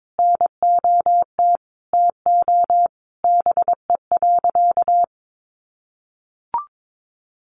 Morse code, heard as a keyed tone, about 22 wpm, 700 Hz. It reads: NOT TO BE.
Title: Sentences from Top 600 Words 22wpm